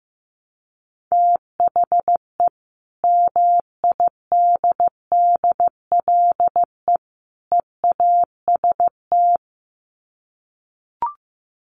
Morse code, heard as a keyed tone, about 15 wpm, 700 Hz. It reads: THE MIDDLE EAST
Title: Sets of 3 Words 15wpm